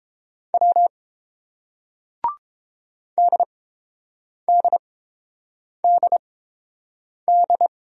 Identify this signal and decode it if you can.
Morse code, heard as a keyed tone, about 29 wpm, spaced as for 7 wpm, 700 Hz. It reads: W DDDD